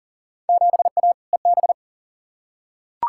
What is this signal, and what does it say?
Morse code, heard as a keyed tone, about 40 wpm, 700 Hz. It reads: 7A EB
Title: Field Day Exchanges Encore - 40wpm